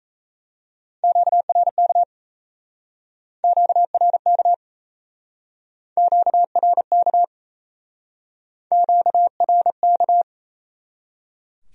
Morse code, heard as a keyed tone, about 37 wpm, 700 Hz. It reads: QRK QRK QRK QRK